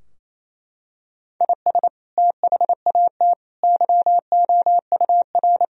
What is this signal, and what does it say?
Morse code, heard as a keyed tone, about 27 wpm, 700 Hz. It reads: IS THAT YOUR
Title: Sentences from Top 1000 Words - Review Part 2 - 28wpm